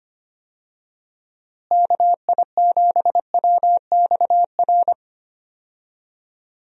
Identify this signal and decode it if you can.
Morse code, heard as a keyed tone, about 25 wpm, 700 Hz. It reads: KI7WXR
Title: USA Call Signs - 6 Characters - Repeated 3x - 25wpm